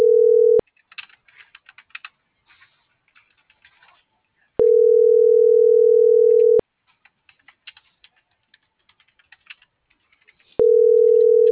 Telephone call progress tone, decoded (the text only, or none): ringback tone